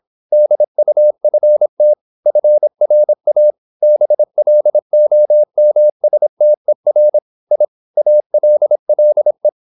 Morse code, heard as a keyed tone, about 26 wpm, 600 Hz. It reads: DUFT FRA BLOMSTER I ALLE
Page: Ferie 26 wpm